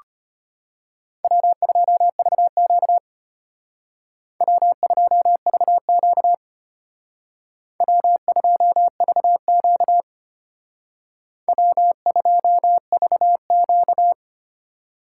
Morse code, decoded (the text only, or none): W2VQ W2VQ W2VQ W2VQ